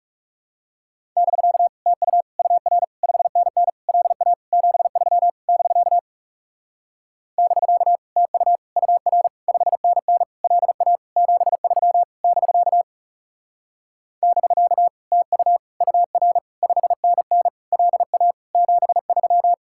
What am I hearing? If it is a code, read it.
Morse code, heard as a keyed tone, about 37 wpm, 700 Hz. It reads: <BK> TU UR 5NN LA 73 <BK> <BK> TU UR 5NN LA 73 <BK> <BK> TU UR 5NN LA 73
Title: Speed-Racing Common QSO Phrases Encore - 30wpm